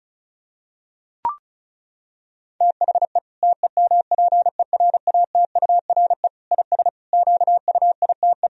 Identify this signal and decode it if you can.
Morse code, heard as a keyed tone, about 34 wpm, 700 Hz. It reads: THE TEMPERATURE IS QUITE
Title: Sentences from Top 1000 Words - Review Part 4 - 35wpm